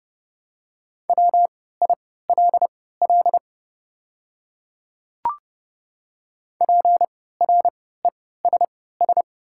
Morse code, heard as a keyed tone, about 30 wpm, spaced as for 17 wpm, 700 Hz. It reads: WILL PRESS
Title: 3x Character Spacing - Top 500 Words - 30wpm